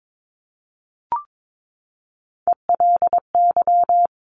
Morse code, heard as a keyed tone, about 22 wpm, 700 Hz. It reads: ELY
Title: Top Three Letter Combinations in Words 22wpm